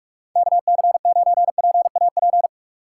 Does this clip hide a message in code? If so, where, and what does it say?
Yes: Morse code from 0.4 to 2.5 s, heard as a keyed tone, about 41 wpm, 700 Hz. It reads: KC9PAP